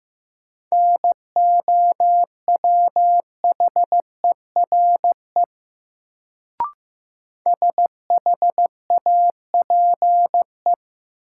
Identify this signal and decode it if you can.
Morse code, heard as a keyed tone, about 15 wpm, 700 Hz. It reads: NOWHERE SHAPE
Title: Sets of 5 Words Encore - New Word Drill - 15wpm